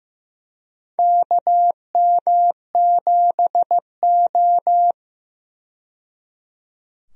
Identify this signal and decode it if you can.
Morse code, heard as a keyed tone, about 15 wpm, 700 Hz. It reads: KM7O
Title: USA Call Signs - 4 Characters - 15wpm